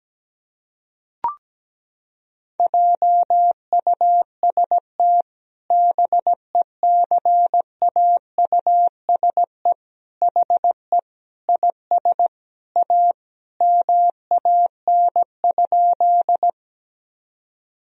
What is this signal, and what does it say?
Morse code, heard as a keyed tone, about 17 wpm, 700 Hz. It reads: JUST BECAUSE HE IS A MAN?
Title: Sentences from Top 1000 Words - Review Part 3 - 17wpm